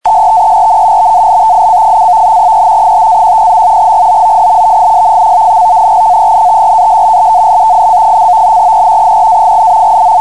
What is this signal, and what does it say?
NAVTEX (SITOR-B), about 100 baud. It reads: STERN BALTIC: E 5-10. GOOD VIS. CENTRAL BALTIC: E 4-8,
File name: Navtex 518kHz